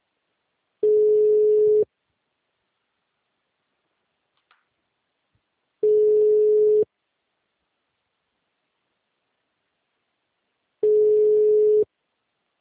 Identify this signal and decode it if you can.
Ringback tone (Europe, 425 Hz).